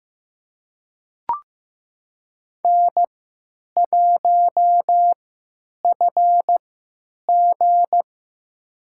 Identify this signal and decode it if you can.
Morse code, heard as a keyed tone, about 15 wpm, spaced as for 9 wpm, 700 Hz. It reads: N1FG
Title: USA Call Signs - 4 Characters - 3x Character Spacing - 15wpm